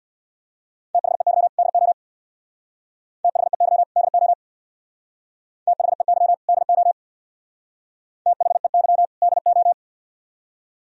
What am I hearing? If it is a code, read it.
Morse code, heard as a keyed tone, about 65 wpm, 700 Hz. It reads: THEY DO THEY DO THEY DO THEY DO